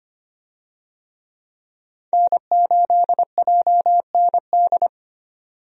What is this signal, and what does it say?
Morse code, heard as a keyed tone, about 25 wpm, 700 Hz. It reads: N8JND
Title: USA Call Signs - 5 Characters - 25wpm